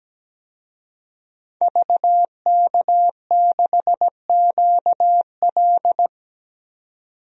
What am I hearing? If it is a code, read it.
Morse code, heard as a keyed tone, about 17 wpm, 700 Hz. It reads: VK6QL
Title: DXCC-100 Call Signs - 5 Characters - 17wpm